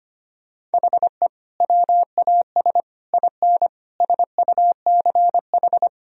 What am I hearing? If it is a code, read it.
Morse code, heard as a keyed tone, about 25 wpm, 700 Hz. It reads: HE WAS IN SUCH
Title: Sentences from Top 1000 Words - Review Part 2 - 25wpm